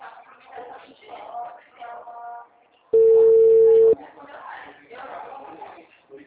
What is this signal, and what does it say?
Ringback tone (China, 450 Hz), with speech.